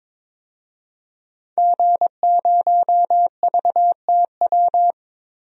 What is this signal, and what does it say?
Morse code, heard as a keyed tone, about 22 wpm, 700 Hz. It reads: G0VTW
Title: DXCC-100 Call Signs - 5 Characters - 22wpm